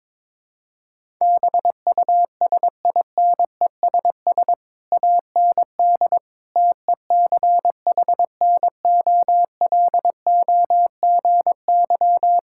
Morse code, heard as a keyed tone, about 22 wpm, 700 Hz. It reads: BUSINESS AND TECHNOLOGY
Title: Sentences from Top 1000 Words - Review Part 5 - 22wpm